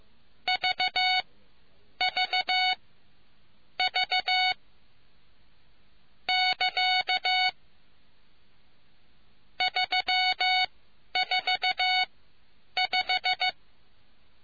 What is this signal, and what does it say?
Morse code, heard as a keyed tone, about 15 wpm, spaced as for 9 wpm, 2950 Hz. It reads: VV V <KA> 345